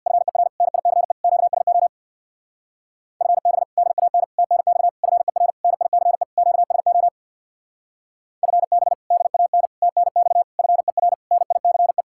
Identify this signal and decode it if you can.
Morse code, heard as a keyed tone, about 66 wpm, 700 Hz. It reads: FER NICE QSO FB DAN TNX FER NICE QSO FB DAN TNX FER NICE